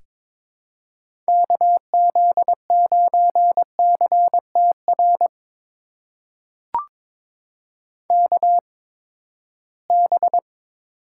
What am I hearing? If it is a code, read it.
Morse code, heard as a keyed tone, about 22 wpm, spaced as for 15 wpm, 700 Hz. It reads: KZ9CTR KB
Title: USA Call Signs - 6 Characters - 8x Character Spacing - 22wpm